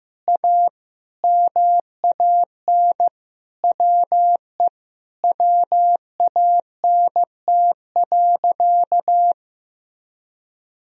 Morse code, heard as a keyed tone, about 15 wpm, 700 Hz. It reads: A MAN WE WANT.
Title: Sentences from Top 200 Words and Abbreviations 15wpm